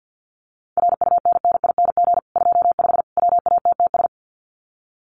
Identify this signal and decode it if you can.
Morse code, heard as a keyed tone, about 50 wpm, 700 Hz. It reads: RUNNING 25 WATTS
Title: Common QSO Phrases 50wpm